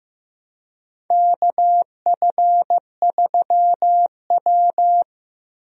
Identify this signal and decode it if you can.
Morse code, heard as a keyed tone, about 15 wpm, 700 Hz. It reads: KF3W
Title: USA Call Signs - 4 Characters - 15wpm